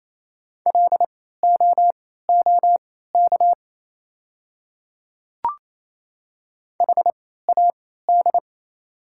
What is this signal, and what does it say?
Morse code, heard as a keyed tone, about 28 wpm, spaced as for 16 wpm, 690 Hz. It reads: LOOK HAD
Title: 3x Character Spacing - Top 100 Words - 28wpm